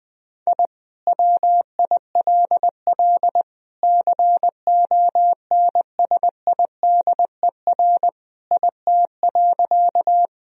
Morse code, heard as a keyed tone, about 20 wpm, 700 Hz. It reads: I WILL CONSIDER IT.
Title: Sentences from Top 1000 Words - Review Part 4 - 20wpm